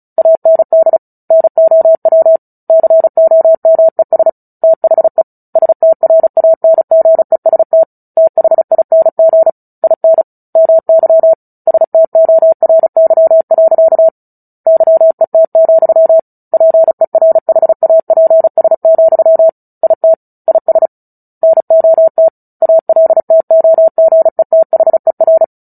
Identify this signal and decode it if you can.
Morse code, heard as a keyed tone, about 34 wpm, 650 Hz. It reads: AND NOW COMES THE STRANGEST THING IN MY STORY. YET, PERHAPS, IT IS NOT ALTOGETHER
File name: Book Two / 09-Chapter 9 - 35 WPM